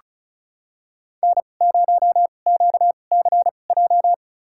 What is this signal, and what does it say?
Morse code, heard as a keyed tone, about 36 wpm, spaced as for 25 wpm, 700 Hz. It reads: N0QCJ
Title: USA Call Signs - 5 Characters - 2x Character Spacing - 35wpm